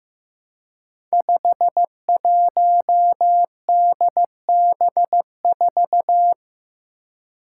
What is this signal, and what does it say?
Morse code, heard as a keyed tone, about 15 wpm, 700 Hz. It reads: 51DB4
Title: Five Challenging Letter-Number Combinations 15wpm